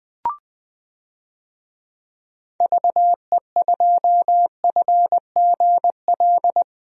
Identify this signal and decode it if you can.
Morse code, heard as a keyed tone, about 20 wpm, 700 Hz. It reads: VE2FGL
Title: DXCC-100 Call Signs - 6 Characters - Repeated 3x - 20wpm